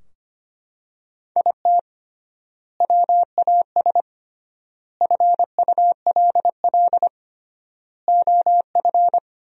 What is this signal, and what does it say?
Morse code, heard as a keyed tone, about 25 wpm, 700 Hz. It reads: IT WAS FULL OF
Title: 3x Word-Spacing - Sentences from Top 500 Words - 25wpm